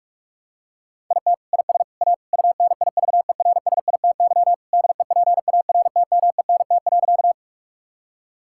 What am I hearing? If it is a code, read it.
Morse code, heard as a keyed tone, about 44 wpm, 700 Hz. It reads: IT IS A UNIVERSITY DEPARTMENT.
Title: Sentences from Top 1000 Words - Review Part 5 - 45wpm